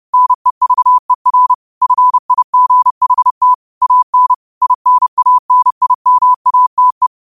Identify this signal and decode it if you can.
Morse code, heard as a keyed tone, about 30 wpm, 1000 Hz. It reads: NEVER FIGHT AN INANIMATE
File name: qotdfiles / 30wpm